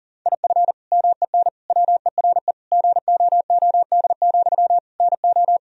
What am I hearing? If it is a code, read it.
Morse code, heard as a keyed tone, about 40 wpm, 700 Hz. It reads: IF MEN WERE GOOD, NO
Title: Sentences from Top 1000 Words - Review Part 3 - 40wpm